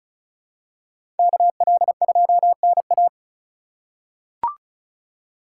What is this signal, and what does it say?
Morse code, heard as a keyed tone, about 36 wpm, 700 Hz. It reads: KL2NA
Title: USA Call Signs - 2x Character Spacing - 35wpm